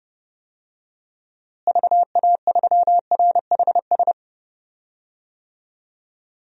Morse code, heard as a keyed tone, about 30 wpm, 700 Hz. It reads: VA3RHS
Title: DXCC-100 Call Signs - 6 Characters - Repeated 2x - 30wpm